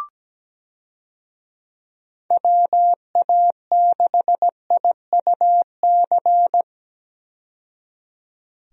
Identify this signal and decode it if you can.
Morse code, heard as a keyed tone, about 17 wpm, 700 Hz. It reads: WA6IUC
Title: USA Call Signs - 6 Characters - 17wpm